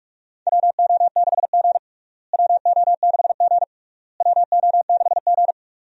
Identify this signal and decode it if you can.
Morse code, heard as a keyed tone, about 43 wpm, 700 Hz. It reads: WO6G WO6G WO6G
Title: USA Call Signs - 4 Characters - Repeated 3x - 45wpm